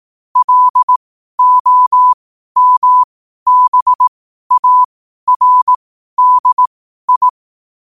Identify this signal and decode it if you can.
Morse code, heard as a keyed tone, about 18 wpm, spaced as for 12 wpm, 1000 Hz. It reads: LOMBARDI
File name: qotdfiles / 12wpm